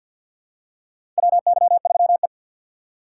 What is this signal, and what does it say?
Morse code, heard as a keyed tone, about 52 wpm, 700 Hz. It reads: WY3E
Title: USA Call Signs - 2x Character Spacing - 50wpm